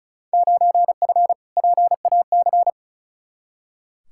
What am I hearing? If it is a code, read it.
Morse code, heard as a keyed tone, about 36 wpm, 690 Hz. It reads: 9F PAC
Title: Field Day Exchanges Encore - 35wpm